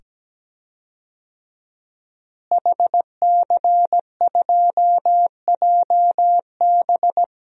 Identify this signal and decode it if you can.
Morse code, heard as a keyed tone, about 17 wpm, 700 Hz. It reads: HC2JB
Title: DXCC-100 Call Signs - 5 Characters - Repeated 2x - 17wpm